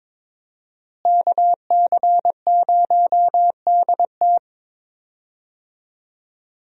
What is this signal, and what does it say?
Morse code, heard as a keyed tone, about 22 wpm, 700 Hz. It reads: KC0DT
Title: USA Call Signs - 5 Characters - Repeated 3x - 22wpm